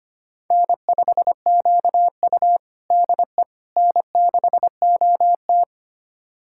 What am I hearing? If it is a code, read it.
Morse code, heard as a keyed tone, about 25 wpm, 700 Hz. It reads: N5QU DE N6OT
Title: USA Callsign DE Callsign - 4 Characters - 25wpm